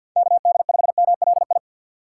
Morse code, heard as a keyed tone, about 48 wpm, 700 Hz. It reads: K D 5 M L I
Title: USA Call Signs - 6 Characters - 8x Character Spacing - 50wpm